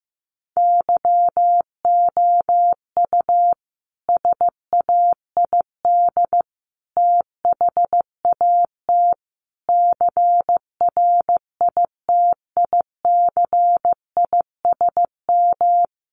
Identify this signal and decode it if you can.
Morse code, heard as a keyed tone, about 15 wpm, 700 Hz. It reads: YOU SAID THAT CRITICISM
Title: Sentences from Top 1300 Words 15wpm